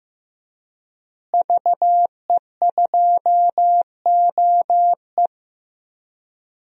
Morse code, heard as a keyed tone, about 15 wpm, 700 Hz. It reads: VE2OE
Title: DXCC-100 Call Signs - 5 Characters - Repeated 2x - 15wpm